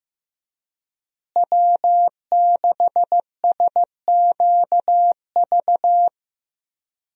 Morse code, heard as a keyed tone, about 15 wpm, 700 Hz. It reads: W6SQV
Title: USA Call Signs - 5 Characters - Repeated 3x - 15wpm